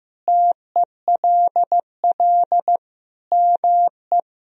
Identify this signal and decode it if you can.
Morse code, heard as a keyed tone, about 15 wpm, 700 Hz. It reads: TELL ME
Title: Sentences from Top 1000 Words - Review Part 3 - 15wpm